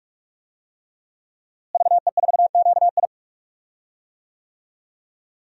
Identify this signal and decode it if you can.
Morse code, heard as a keyed tone, about 46 wpm, 700 Hz. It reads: VE4QI
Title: DXCC-100 Call Signs - 5 Characters - Repeated 3x - 45wpm